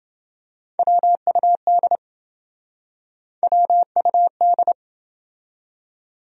Morse code, heard as a keyed tone, about 28 wpm, 700 Hz. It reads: WUD WUD